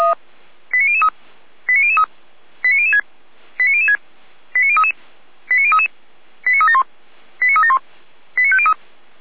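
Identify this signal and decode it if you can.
Five-tone selective call (ZVEI), heard as quick runs of five tones, tones of about 80 ms. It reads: 89003 89003 89006 89006 89033 89033 89371 89371 89503
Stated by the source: Five-tone ZVEI selective call 89099 received by SDR, from audio.f85360.feuerwehr-orlamuende.org fire brigade server.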